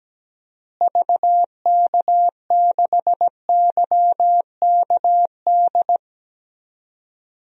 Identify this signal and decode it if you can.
Morse code, heard as a keyed tone, about 17 wpm, 700 Hz. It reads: VK6YKD
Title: DXCC-100 Call Signs - 6 Characters - Repeated 2x - 17wpm